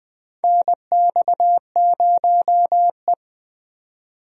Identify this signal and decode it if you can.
Morse code, heard as a keyed tone, about 20 wpm, 700 Hz. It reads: NX0E
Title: USA Call Signs - 4 Characters - 20wpm